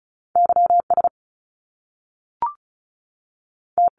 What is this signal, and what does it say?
Morse code, heard as a keyed tone, about 36 wpm, spaced as for 6 wpm, 700 Hz. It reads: YST
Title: Top Two Letter Combinations in Words 35wpm